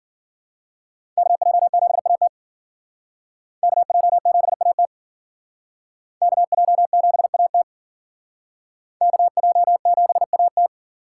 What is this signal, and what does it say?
Morse code, heard as a keyed tone, about 50 wpm, 700 Hz. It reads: KJ7AT KJ7AT KJ7AT KJ7AT